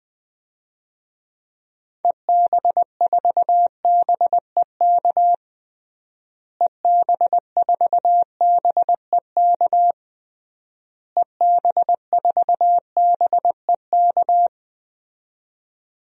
Morse code, heard as a keyed tone, about 20 wpm, 700 Hz. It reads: EB4BEK EB4BEK EB4BEK
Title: DXCC-100 Call Signs - 6 Characters - Repeated 3x - 20wpm